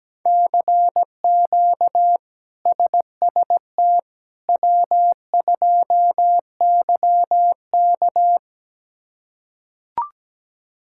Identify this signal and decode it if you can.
Morse code, heard as a keyed tone, about 17 wpm, 700 Hz. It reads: CQ SST W2YK
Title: K1USN SST QSO Practice - 17wpm